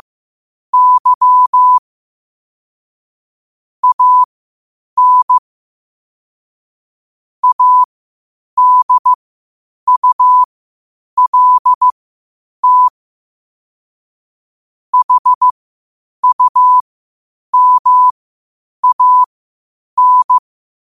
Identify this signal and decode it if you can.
Morse code, heard as a keyed tone, about 15 wpm, spaced as for 8 wpm, 1000 Hz. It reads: Y AN ADULT HUMAN
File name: qotdfiles / 8wpm